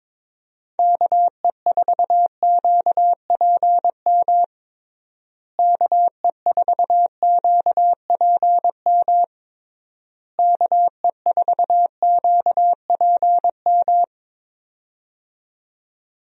Morse code, heard as a keyed tone, about 22 wpm, 700 Hz. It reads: KE4QPM KE4QPM KE4QPM
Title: USA Call Signs - 6 Characters - Repeated 3x - 22wpm